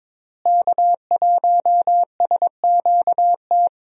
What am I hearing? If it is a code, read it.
Morse code, heard as a keyed tone, about 22 wpm, 700 Hz. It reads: K1SQT
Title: USA Call Signs - 5 Characters - Repeated 3x - 22wpm